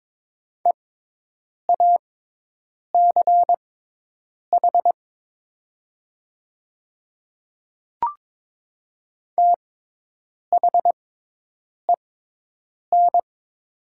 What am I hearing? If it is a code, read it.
Morse code, heard as a keyed tone, about 22 wpm, spaced as for 8 wpm, 700 Hz. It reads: EACH THEN
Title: 6x Character Spacing - Top 100 Words - 22wpm